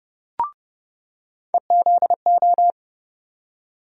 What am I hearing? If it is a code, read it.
Morse code, heard as a keyed tone, about 30 wpm, 700 Hz. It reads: EZO
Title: USA Call Sign Postfixes - 3 Letters 30wpm